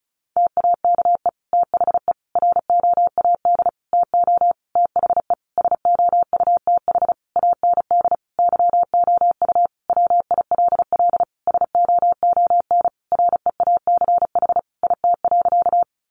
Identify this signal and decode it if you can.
Morse code, heard as a keyed tone, about 34 wpm, 700 Hz. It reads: TAKE THE ROAD TO THE SOUTH AND YOU WILL SOON REACH IT.
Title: Sentences from Top 400 Words 35wpm